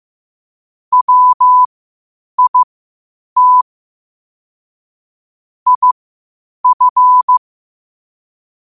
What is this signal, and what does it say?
Morse code, heard as a keyed tone, about 15 wpm, spaced as for 8 wpm, 1000 Hz. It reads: WIT IF
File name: qotdfiles / 8wpm